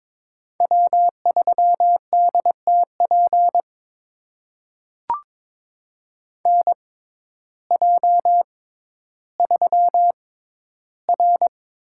Morse code, heard as a keyed tone, about 22 wpm, spaced as for 12 wpm, 700 Hz. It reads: W3DTP NJ3R
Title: USA Call Signs - 6x Character Spacing - 22wpm